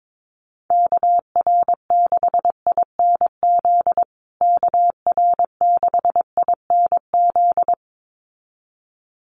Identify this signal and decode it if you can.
Morse code, heard as a keyed tone, about 22 wpm, 700 Hz. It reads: KR6INZ KR6INZ
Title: USA Call Signs Repeated 2x 22wpm